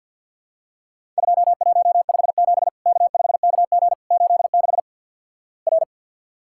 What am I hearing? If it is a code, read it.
Morse code, heard as a keyed tone, about 50 wpm, 700 Hz. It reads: 2157 K5KG 86 R
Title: Worked All Europe DX Contest - QTC Exchange - 50wpm